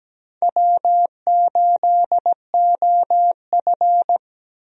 Morse code, heard as a keyed tone, about 17 wpm, 700 Hz. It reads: W8OF
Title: USA Call Signs - 4 Characters - 17wpm